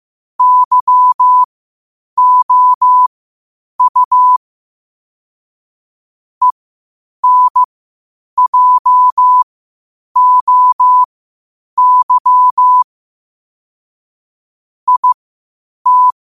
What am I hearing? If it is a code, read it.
Morse code, heard as a keyed tone, about 15 wpm, spaced as for 8 wpm, 1000 Hz. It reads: YOU ENJOY IT
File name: qotdfiles / 8wpm